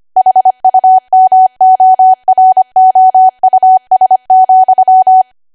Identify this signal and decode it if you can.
Morse code, heard as a keyed tone, about 25 wpm, 750 Hz. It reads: HUMOROUS,
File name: Archive / 25 WPM / 25WPM